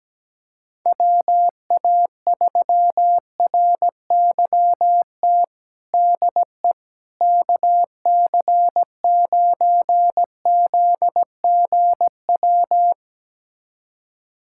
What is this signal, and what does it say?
Morse code, heard as a keyed tone, about 17 wpm, 700 Hz. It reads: WA3RYT DE KC9ZGW
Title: USA Callsign DE Callsign 17wpm